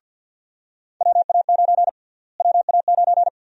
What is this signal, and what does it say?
Morse code, heard as a keyed tone, about 50 wpm, 700 Hz. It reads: WA9 WA9
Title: USA Call Sign Prefixes - Repeated 2x - 50wpm